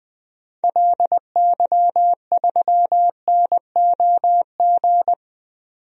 Morse code, heard as a keyed tone, about 20 wpm, 700 Hz. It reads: LY3NOG
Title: DXCC-100 Call Signs - 6 Characters - 20wpm